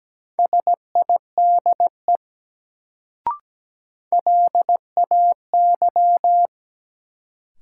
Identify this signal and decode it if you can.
Morse code, heard as a keyed tone, about 17 wpm, 700 Hz. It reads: SIDE LAY
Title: Sentences from Top 400 Words - New Word Drill - 17wpm